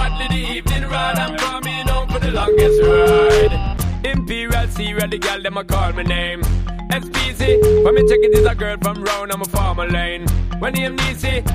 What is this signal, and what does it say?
Ringback tone (Europe, 425 Hz), with speech.